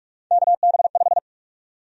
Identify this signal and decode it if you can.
Morse code, heard as a keyed tone, about 45 wpm, 700 Hz. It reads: KB5
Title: USA Call Sign Prefixes - Repeated 2x - 45wpm